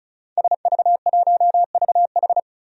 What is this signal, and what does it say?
Morse code, heard as a keyed tone, about 36 wpm, 700 Hz. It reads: SV1VH
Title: DXCC-100 Call Signs - 5 Characters - Repeated 2x - 35wpm